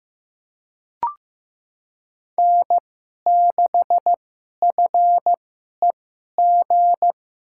Morse code, heard as a keyed tone, about 15 wpm, spaced as for 11 wpm, 700 Hz. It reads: N6FEG
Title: USA Call Signs - 5 Characters - 2x Character Spacing - 15wpm